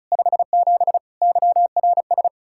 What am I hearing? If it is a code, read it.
Morse code, heard as a keyed tone, about 43 wpm, 700 Hz. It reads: 57 YRS
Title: Speed-Racing Common QSO Phrases Encore - 35wpm